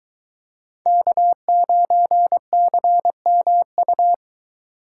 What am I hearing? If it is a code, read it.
Morse code, heard as a keyed tone, about 22 wpm, 700 Hz. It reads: K9CMU